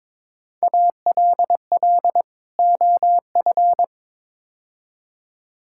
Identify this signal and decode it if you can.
Morse code, heard as a keyed tone, about 22 wpm, 700 Hz. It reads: ALL OF
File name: practice / 22wpm / Build A Phrase - 5-Word Phrases - 22wpm